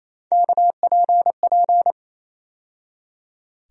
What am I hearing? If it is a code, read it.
Morse code, heard as a keyed tone, about 28 wpm, 700 Hz. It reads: KPP
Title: USA Call Sign Postfixes - 3 Letters 28wpm